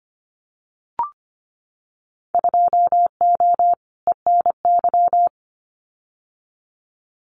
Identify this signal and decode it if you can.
Morse code, heard as a keyed tone, about 25 wpm, 700 Hz. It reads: 2O ENY
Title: Winter Field Day Exchanges 25wpm